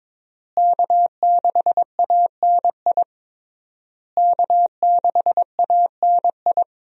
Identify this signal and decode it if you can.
Morse code, heard as a keyed tone, about 22 wpm, 700 Hz. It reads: K6ANI K6ANI
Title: USA Call Signs - 5 Characters - Repeated 3x - 22wpm